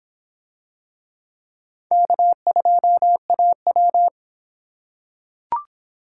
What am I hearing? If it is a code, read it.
Morse code, heard as a keyed tone, about 24 wpm, 700 Hz. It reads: K2AW